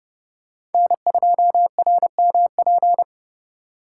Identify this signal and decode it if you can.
Morse code, heard as a keyed tone, about 30 wpm, 700 Hz. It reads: N2RMP
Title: USA Call Signs - 5 Characters - 2x Character Spacing - 30wpm